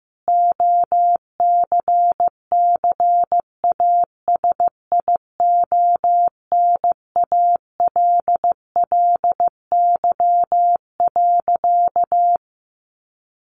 Morse code, heard as a keyed tone, about 15 wpm, 700 Hz. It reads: OCCASIONALLY.
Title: Sentences from Top 1300 Words 15wpm